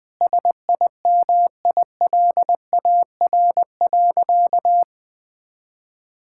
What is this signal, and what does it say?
Morse code, heard as a keyed tone, about 20 wpm, 700 Hz. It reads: SIMILAR.
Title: Sentences from Top 1000 Words - Review Part 5 - 20wpm